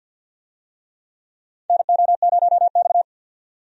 Morse code, heard as a keyed tone, about 50 wpm, 700 Hz. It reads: NO0X
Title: USA Call Signs - 4 Characters - 50wpm